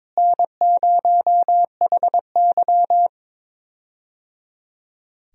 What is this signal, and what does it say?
Morse code, heard as a keyed tone, about 21 wpm, 700 Hz. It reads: N0HY
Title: USA Call Signs - 4 Characters - Repeated 3x - 22wpm